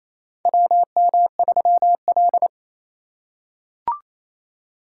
Morse code, heard as a keyed tone, about 28 wpm, 700 Hz. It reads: WM3L
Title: USA Call Signs - 4 Characters - 2x Character Spacing - 28wpm